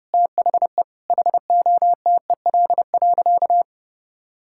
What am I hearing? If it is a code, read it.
Morse code, heard as a keyed tone, about 30 wpm, 700 Hz. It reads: THE HOTEL.
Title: Sentences from Top 1600 Words 30wpm